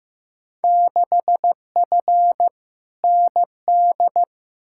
Morse code, heard as a keyed tone, about 15 wpm, 700 Hz. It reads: 6F ND
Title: Field Day Exchanges Encore - 15wpm